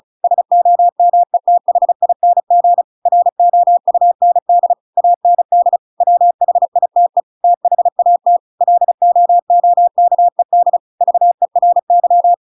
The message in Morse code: SOMETHING ROUND AND WHITE THAT LOOKED VERY

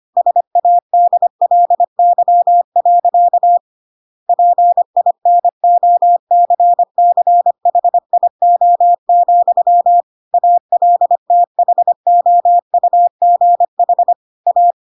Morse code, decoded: SADLY. PINOCCHIO, ALTHOUGH A

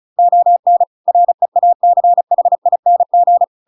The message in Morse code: ON REACHING

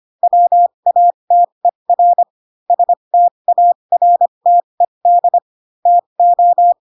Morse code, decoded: WATER STARTED TO

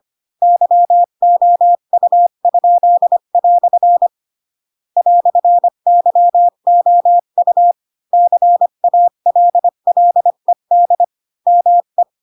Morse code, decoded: YOU?" "YOU CALLED ME

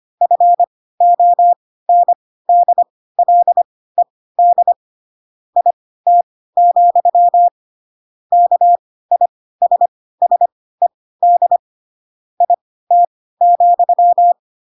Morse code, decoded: FONDLED IT, KISSED IT,

